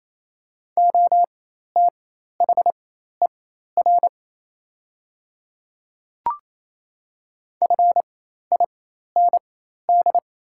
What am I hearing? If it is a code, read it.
Morse code, heard as a keyed tone, about 28 wpm, spaced as for 13 wpm, 700 Hz. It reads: OTHER FIND
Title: 4x Character Spacing - Top 100 Words - 28wpm